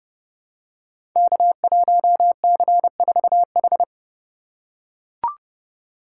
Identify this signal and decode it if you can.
Morse code, heard as a keyed tone, about 30 wpm, 700 Hz. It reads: K1C4H
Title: Code Groups - 5 Letter:Numbers - 30wpm